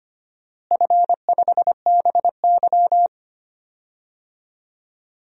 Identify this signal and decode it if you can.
Morse code, heard as a keyed tone, about 25 wpm, 700 Hz. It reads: F5BY
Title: DXCC-100 Call Signs - 4 Characters - Repeated 3x - 25wpm